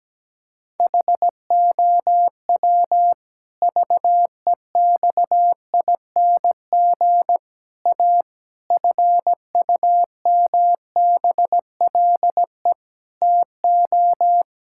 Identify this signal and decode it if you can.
Morse code, heard as a keyed tone, about 17 wpm, 700 Hz. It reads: HOW VEXING A FUMBLE TO
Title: Pangram Challenge v2 - 17wpm